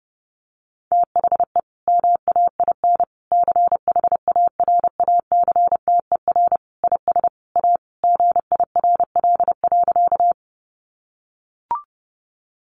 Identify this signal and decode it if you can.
Morse code, heard as a keyed tone, about 30 wpm, 700 Hz. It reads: THE MAIN CHARACTER IS A GIRL.
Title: Additional Easy Phrases 30wpm